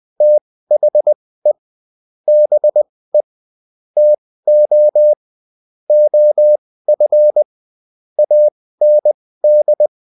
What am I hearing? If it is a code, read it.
Morse code, heard as a keyed tone, about 20 wpm, spaced as for 15 wpm, 600 Hz. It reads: THE BE TO OF AND
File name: Top-100-Words-15